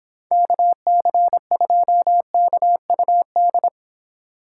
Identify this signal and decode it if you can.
Morse code, heard as a keyed tone, about 26 wpm, 700 Hz. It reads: KC2KUD